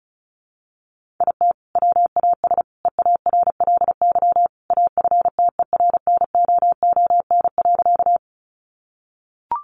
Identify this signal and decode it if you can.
Morse code, heard as a keyed tone, about 34 wpm, 700 Hz. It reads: IT WAS EARLY AFTERNOON.
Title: Sentences from Top 1500 Words 35wpm